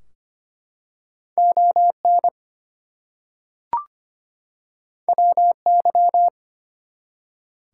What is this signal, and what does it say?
Morse code, heard as a keyed tone, about 25 wpm, 700 Hz. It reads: ON WY
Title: US States and Canadian Provinces 25wpm